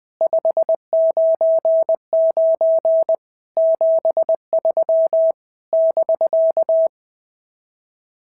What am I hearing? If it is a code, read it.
Morse code, heard as a keyed tone, about 19 wpm, 650 Hz. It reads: 599 73 <BK>
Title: SOTA QSO Practice 20wpm